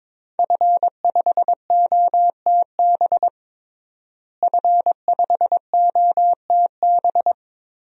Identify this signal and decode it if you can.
Morse code, heard as a keyed tone, about 22 wpm, 700 Hz. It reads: F5OTB F5OTB
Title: DXCC-100 Call Signs - 5 Characters - Repeated 3x - 22wpm